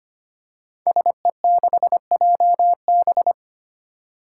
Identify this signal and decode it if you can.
Morse code, heard as a keyed tone, about 25 wpm, 700 Hz. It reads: SE6JB
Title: Five Challenging Letter-Number Combinations 25wpm